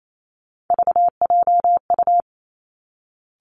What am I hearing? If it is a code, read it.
Morse code, heard as a keyed tone, about 28 wpm, 700 Hz. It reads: VJU
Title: Three Challenging Letter-Number Combinations 28wpm